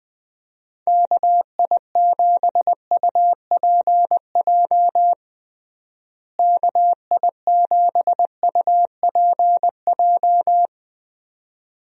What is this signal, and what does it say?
Morse code, heard as a keyed tone, about 20 wpm, 700 Hz. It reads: KI7UPJ KI7UPJ
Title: USA Call Signs - 6 Characters - Repeated 2x - 20wpm